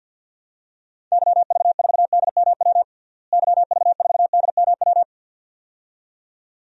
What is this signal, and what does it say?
Morse code, heard as a keyed tone, about 53 wpm, 700 Hz. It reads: YV4DMW YV4DMW
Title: DXCC-100 Call Signs - 6 Characters - Repeated 2x - 50wpm